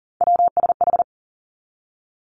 Morse code, heard as a keyed tone, about 41 wpm, 700 Hz. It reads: WSH
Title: Three Challenging Letter-Number Combinations 40wpm